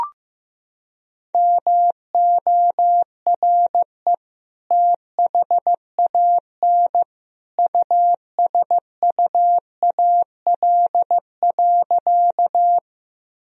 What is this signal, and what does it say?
Morse code, heard as a keyed tone, about 15 wpm, 700 Hz. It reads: MORE THAN USUAL.
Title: Sentences from Top 1000 Words - Review Part 2 - 15wpm